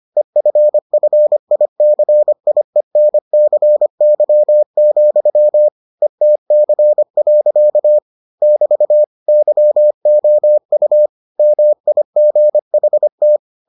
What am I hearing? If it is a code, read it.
Morse code, heard as a keyed tone, about 25 wpm, 600 Hz. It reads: EFFICIENCY, ETC. <BT> YOU MIGHT